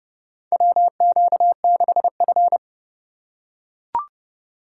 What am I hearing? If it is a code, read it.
Morse code, heard as a keyed tone, about 29 wpm, 700 Hz. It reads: WQ6F
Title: USA Call Signs - 4 Characters - 2x Character Spacing - 30wpm